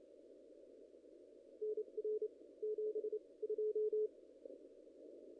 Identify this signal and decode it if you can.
Morse code, heard as a keyed tone, about 28 wpm, spaced as for 23 wpm, 440 Hz. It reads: NR 72
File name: contests / ss-cw-k5tr-op-n5rz -0000